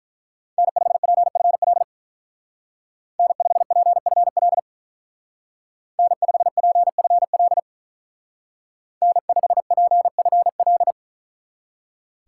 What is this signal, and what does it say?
Morse code, heard as a keyed tone, about 43 wpm, 700 Hz. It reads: N5PFL N5PFL N5PFL N5PFL
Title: Speed Racing - USA 1x3 Call Signs 35wpm